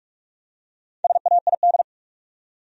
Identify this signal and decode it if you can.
Morse code, heard as a keyed tone, about 46 wpm, 700 Hz. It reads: SAID
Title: Top 100 Words 45wpm